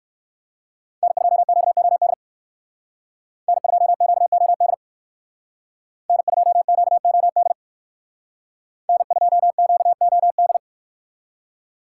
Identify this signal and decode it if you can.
Morse code, heard as a keyed tone, about 58 wpm, 700 Hz. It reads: N2QOD N2QOD N2QOD N2QOD